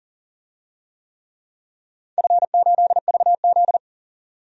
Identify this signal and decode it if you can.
Morse code, heard as a keyed tone, about 40 wpm, 700 Hz. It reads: F8VZ
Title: DXCC-100 Call Signs - 4 Characters - 40wpm